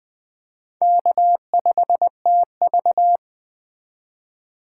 Morse code, heard as a keyed tone, about 21 wpm, 700 Hz. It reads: K5TV